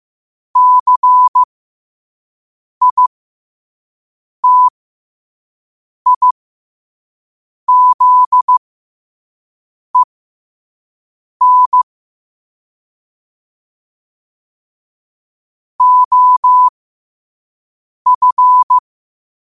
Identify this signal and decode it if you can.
Morse code, heard as a keyed tone, about 15 wpm, spaced as for 5 wpm, 1000 Hz. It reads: CITIZEN OF
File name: qotdfiles / 5wpm